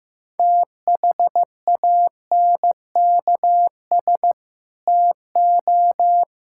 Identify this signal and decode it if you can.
Morse code, heard as a keyed tone, about 15 wpm, 700 Hz. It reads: THANKS TO
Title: Sentences from Top 1000 Words - Review Part 5 - 15wpm